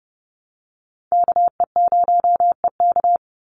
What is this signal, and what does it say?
Morse code, heard as a keyed tone, about 30 wpm, 700 Hz. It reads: KE0EK
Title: Call Signs - 30wpm